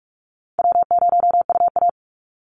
Morse code, heard as a keyed tone, about 44 wpm, 700 Hz. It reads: W0UA
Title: Easy Call Signs Repeated 3x - 45wpm